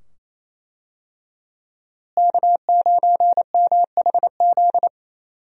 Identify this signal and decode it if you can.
Morse code, heard as a keyed tone, about 28 wpm, 700 Hz. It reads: K9MHZ
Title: USA Call Signs - 5 Characters - Repeated 2x - 28wpm